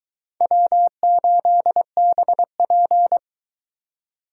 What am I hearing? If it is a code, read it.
Morse code, heard as a keyed tone, about 23 wpm, 700 Hz. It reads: W8BP